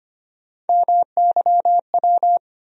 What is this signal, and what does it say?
Morse code, heard as a keyed tone, about 25 wpm, 700 Hz. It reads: MYW
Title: USA Call Sign Postfixes - 3 Letters 25wpm